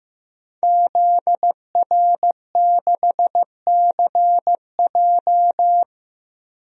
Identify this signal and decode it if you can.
Morse code, heard as a keyed tone, about 15 wpm, 700 Hz. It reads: ZR6CJ
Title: DXCC-100 Call Signs - 5 Characters - Repeated 2x - 15wpm